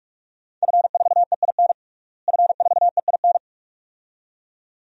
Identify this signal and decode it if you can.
Morse code, heard as a keyed tone, about 42 wpm, 700 Hz. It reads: F4EIN F4EIN
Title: DXCC-100 Call Signs - 5 Characters - Repeated 2x - 45wpm